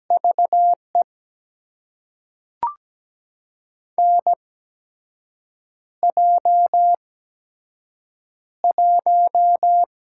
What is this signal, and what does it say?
Morse code, heard as a keyed tone, about 17 wpm, spaced as for 6 wpm, 700 Hz. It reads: VE NJ1
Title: USA Call Signs - 4 Characters - 8x Character Spacing - 17wpm